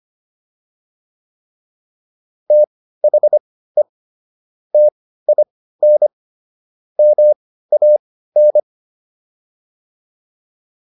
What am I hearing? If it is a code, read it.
Morse code, heard as a keyed tone, about 25 wpm, spaced as for 15 wpm, 600 Hz. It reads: THE TIN MAN